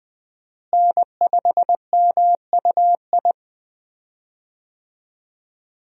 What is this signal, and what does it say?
Morse code, heard as a keyed tone, about 21 wpm, 700 Hz. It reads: N5MUI
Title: USA Call Signs - 5 Characters - Repeated 2x - 20wpm